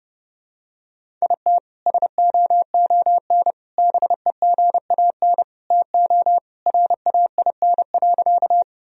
Morse code, heard as a keyed tone, about 32 wpm, 700 Hz. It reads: IT SOON BEGAN TO RAIN.
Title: Sentences from Top 1000 Words - Review Part 1 - 30wpm